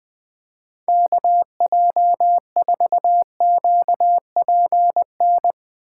Morse code, heard as a keyed tone, about 22 wpm, 700 Hz. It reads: KJ4QPN
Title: Speed Racing - USA 2x3 Call Signs 20wpm